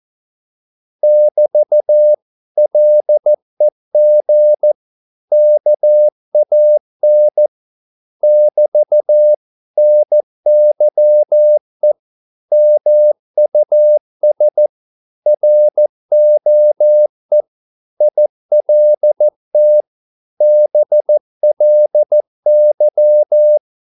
Morse code, heard as a keyed tone, about 14 wpm, 600 Hz. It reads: <BT> LEG KAN <BT> NYE MUS ROE ILT BLY
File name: Korte ord DK 14wpm